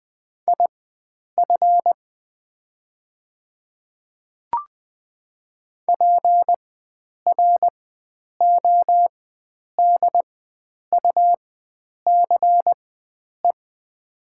Morse code, heard as a keyed tone, about 20 wpm, spaced as for 9 wpm, 700 Hz. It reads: IF PRODUCE
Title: 4x Character Spacing - Top 500 Words - 20wpm